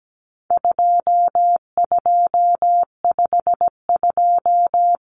Morse code, heard as a keyed tone, about 17 wpm, 700 Hz. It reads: 2252
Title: Common Cut Numbers - 4-Digit 17wpm